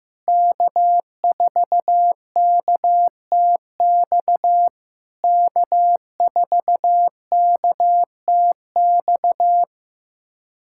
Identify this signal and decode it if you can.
Morse code, heard as a keyed tone, about 15 wpm, 700 Hz. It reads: K4KTX K4KTX
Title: SOTA QSO Practice 15wpm